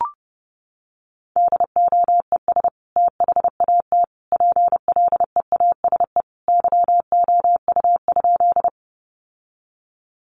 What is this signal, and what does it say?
Morse code, heard as a keyed tone, about 30 wpm, 700 Hz. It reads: DOES THAT PLEASE YOU?
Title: Sentences from Top 1500 Words 30wpm